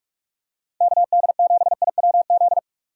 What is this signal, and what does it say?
Morse code, heard as a keyed tone, about 46 wpm, 700 Hz. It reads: KD7IWZ
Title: USA Call Signs - 6 Characters - 45wpm